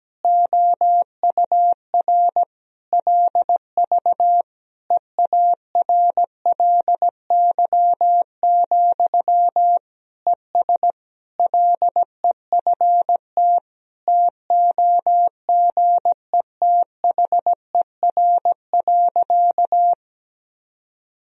Morse code, heard as a keyed tone, about 17 wpm, 700 Hz. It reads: OUR LV EARLY, ES LEFT TOGETHER.
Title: Sentences from Top 400 Words and Abbreviations 17wpm